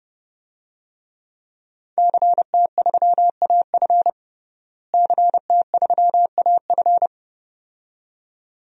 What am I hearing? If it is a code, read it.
Morse code, heard as a keyed tone, about 32 wpm, 700 Hz. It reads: CT3AF CT3AF
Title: DXCC-100 Call Signs - 5 Characters - Repeated 2x - 30wpm